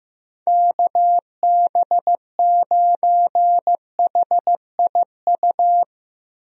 Morse code, heard as a keyed tone, about 15 wpm, 700 Hz. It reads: KB9HIU
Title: USA Call Signs - 6 Characters - Repeated 3x - 15wpm